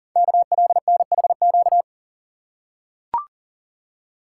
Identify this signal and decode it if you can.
Morse code, heard as a keyed tone, about 40 wpm, 700 Hz. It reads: KLNHQ
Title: Code Groups - 5 Letters - 40wpm